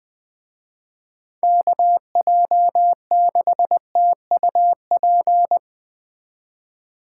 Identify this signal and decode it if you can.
Morse code, heard as a keyed tone, about 20 wpm, 700 Hz. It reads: KJ6TUP